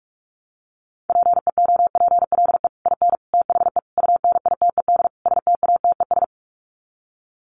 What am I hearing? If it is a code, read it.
Morse code, heard as a keyed tone, about 44 wpm, 700 Hz. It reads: PEOPLE IN THE UNITED STATES
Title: Sets of 5 Words Encore 45wpm